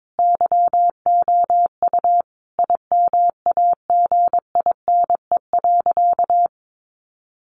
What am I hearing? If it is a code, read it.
Morse code, heard as a keyed tone, about 22 wpm, 700 Hz. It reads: YOU IMAGINE.
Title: Sentences from Top 1400 Words 22wpm